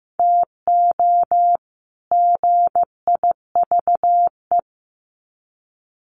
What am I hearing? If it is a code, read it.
Morse code, heard as a keyed tone, about 15 wpm, 700 Hz. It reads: TO GIVE
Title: Sets of 2 Words 15wpm